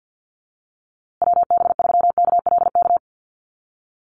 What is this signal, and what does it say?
Morse code, heard as a keyed tone, about 48 wpm, 700 Hz. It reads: WB3KLK
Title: USA Call Signs Repeated 3x 50wpm